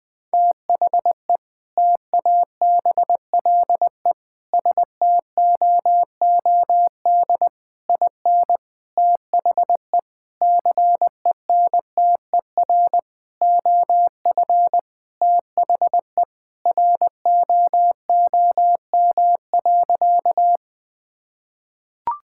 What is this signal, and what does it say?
Morse code, heard as a keyed tone, about 21 wpm, 700 Hz. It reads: THE TABLE STOOD IN THE CENTER OF THE ROOM.
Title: Sentences with 4-Word Phrases - 20wpm